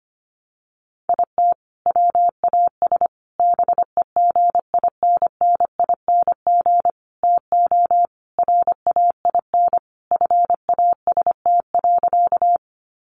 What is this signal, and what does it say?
Morse code, heard as a keyed tone, about 25 wpm, 700 Hz. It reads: IT WAS BEGINNING TO RAIN FAST.
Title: Sentences from Top 800 Words 25wpm